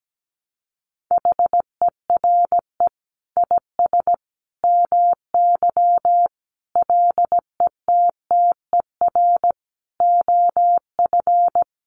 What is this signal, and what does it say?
Morse code, heard as a keyed tone, about 17 wpm, 700 Hz. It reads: HERE IS MY LETTER OF
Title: Sentences from Top 1400 Words 17wpm